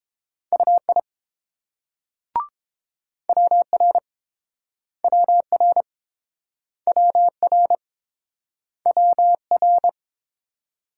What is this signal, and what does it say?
Morse code, heard as a keyed tone, about 30 wpm, 700 Hz. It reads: UI WR WR WR WR